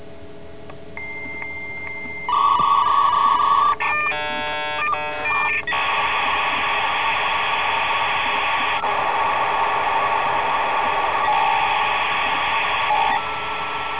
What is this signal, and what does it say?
Dial-up modem handshake.